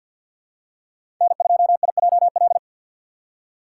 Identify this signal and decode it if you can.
Morse code, heard as a keyed tone, about 47 wpm, 700 Hz. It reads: N2IJL
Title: USA Call Signs - 5 Characters - 4x Character Spacing - 50wpm